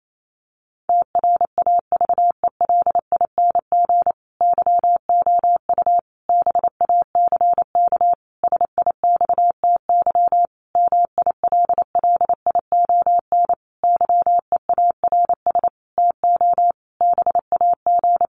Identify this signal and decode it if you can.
Morse code, heard as a keyed tone, about 28 wpm, 700 Hz. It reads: TRAVELING YOU BACK SIXTY MILLION YEARS TO BAG
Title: A Sound Of Thunder 28wpm